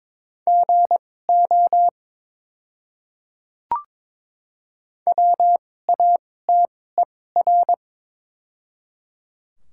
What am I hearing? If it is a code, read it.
Morse code, heard as a keyed tone, about 22 wpm, spaced as for 16 wpm, 700 Hz. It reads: GO WATER
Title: 2x Character Spacing - Top 100 Words - 22wpm